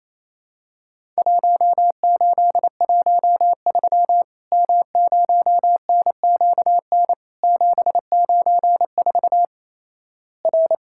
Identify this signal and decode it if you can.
Morse code, heard as a keyed tone, about 28 wpm, 700 Hz. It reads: 1813 M0NQN 794 R
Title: Worked All Europe DX Contest - QTC Exchange - 28wpm